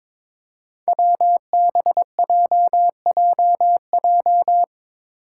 Morse code, heard as a keyed tone, about 22 wpm, 700 Hz. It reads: WBJJJ
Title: Five Challenging Letter-Number Combinations 22wpm